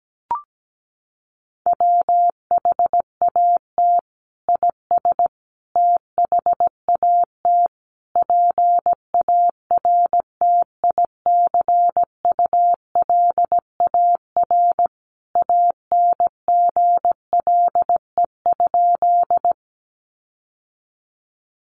Morse code, heard as a keyed tone, about 17 wpm, 700 Hz. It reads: WHAT IS THAT PARTICULAR ANGLE?
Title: Sentences from Top 1500 Words 17wpm